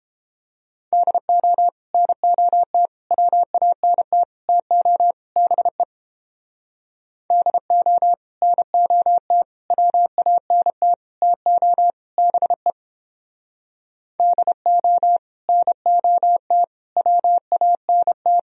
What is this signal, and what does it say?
Morse code, heard as a keyed tone, about 31 wpm, 700 Hz. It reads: DO NOT WANT TO BE DO NOT WANT TO BE DO NOT WANT